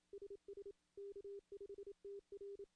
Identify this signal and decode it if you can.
Morse code, heard as a keyed tone, about 26 wpm, 410 Hz. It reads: SS K5TR
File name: CW / CW-SS-k5tr- -1700